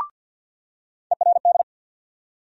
Morse code, heard as a keyed tone, about 50 wpm, 700 Hz. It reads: ERD
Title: USA Call Sign Postfixes - 3 Letters 50wpm